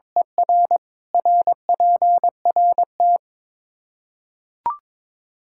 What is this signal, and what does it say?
Morse code, heard as a keyed tone, about 22 wpm, 700 Hz. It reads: ER RPRT
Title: Practice QSO - Exchange 1 - Beginning - 22wpm